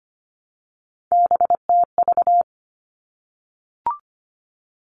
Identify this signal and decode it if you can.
Morse code, heard as a keyed tone, about 25 wpm, 700 Hz. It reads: BTV
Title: All Cut Numbers - 3-Digit 25wpm